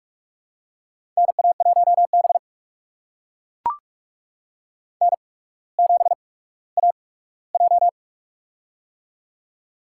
Morse code, heard as a keyed tone, about 44 wpm, spaced as for 19 wpm, 700 Hz. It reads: NA1B N7AJ
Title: USA Call Signs - 4 Characters - 8x Character Spacing - 45wpm